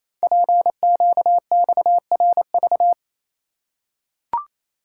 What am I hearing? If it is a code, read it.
Morse code, heard as a keyed tone, about 28 wpm, 700 Hz. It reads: PQXRV
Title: Code Groups - 5 Letters - 28wpm